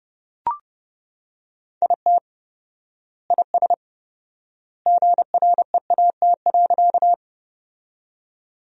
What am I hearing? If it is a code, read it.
Morse code, heard as a keyed tone, about 28 wpm, 700 Hz. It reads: IT IS GREAT.
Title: 4x Word-Spacing - Sentences from Top 200 Words - 30wpm